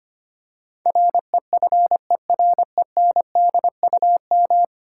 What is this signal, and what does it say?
Morse code, heard as a keyed tone, about 25 wpm, 700 Hz. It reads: REFERENDUM
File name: Ten Letter Words - 25wpm